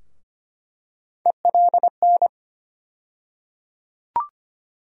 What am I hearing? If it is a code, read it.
Morse code, heard as a keyed tone, about 25 wpm, 700 Hz. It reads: ELN
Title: USA Call Sign Postfixes - 3 Letters 25wpm